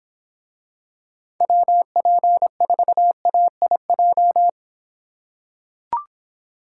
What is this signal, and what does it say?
Morse code, heard as a keyed tone, about 26 wpm, 700 Hz. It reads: WP4AIJ